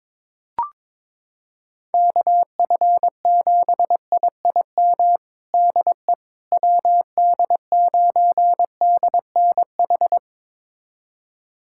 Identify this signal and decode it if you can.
Morse code, heard as a keyed tone, about 22 wpm, 700 Hz. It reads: KF7IIM DE WD9DNH
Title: USA Callsign DE Callsign - 6 Characters - 22wpm